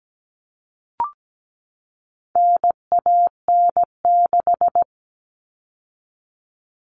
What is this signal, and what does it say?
Morse code, heard as a keyed tone, about 17 wpm, 700 Hz. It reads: NAN6
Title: All Cut Numbers - 4-Digit 17wpm